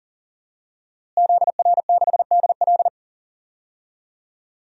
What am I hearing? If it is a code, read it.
Morse code, heard as a keyed tone, about 40 wpm, 700 Hz. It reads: ZR6DL
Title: DXCC-100 Call Signs - 5 Characters - Repeated 3x - 40wpm